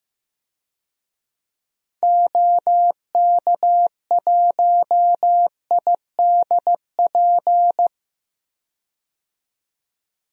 Morse code, heard as a keyed tone, about 15 wpm, 700 Hz. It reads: OK1IDP
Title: DXCC-100 Call Signs - 6 Characters - 15wpm